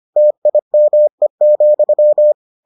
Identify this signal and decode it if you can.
Morse code, heard as a keyed tone, about 25 wpm, 600 Hz. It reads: TIME,